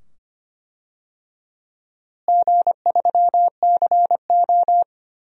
Morse code, heard as a keyed tone, about 25 wpm, 700 Hz. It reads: G3CO
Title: DXCC-100 Call Signs - 4 Characters - Repeated 2x - 25wpm